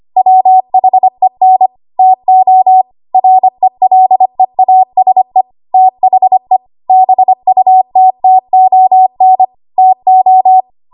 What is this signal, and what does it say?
Morse code, heard as a keyed tone, about 25 wpm, 750 Hz. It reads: WHEN TO RELEASE THE BUTTON TO
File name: Archive / 25 WPM / 25WPM